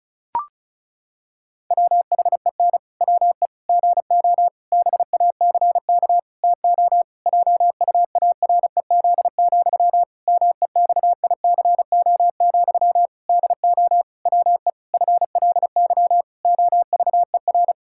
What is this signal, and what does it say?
Morse code, heard as a keyed tone, about 34 wpm, 700 Hz. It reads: WHEN WE GO BACK TO JUAREZ, MEXICO, DO WE FLY OVER
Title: Pangram Challenge v2 - 35wpm